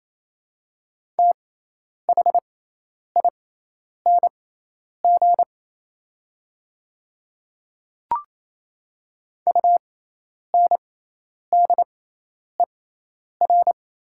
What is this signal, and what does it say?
Morse code, heard as a keyed tone, about 28 wpm, spaced as for 10 wpm, 700 Hz. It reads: THING UNDER
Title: 6x Character Spacing - Top 300 Words - 28wpm